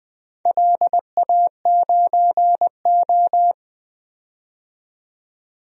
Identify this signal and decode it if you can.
Morse code, heard as a keyed tone, about 20 wpm, 700 Hz. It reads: LA9O
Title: DXCC-100 Call Signs - 4 Characters - 20wpm